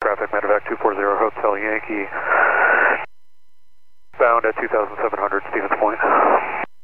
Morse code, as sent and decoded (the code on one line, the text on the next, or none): none